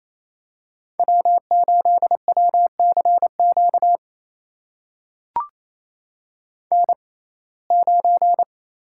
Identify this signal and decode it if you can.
Morse code, heard as a keyed tone, about 28 wpm, spaced as for 20 wpm, 700 Hz. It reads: W8WCQ N9
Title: USA Call Signs - 6x Character Spacing - 28wpm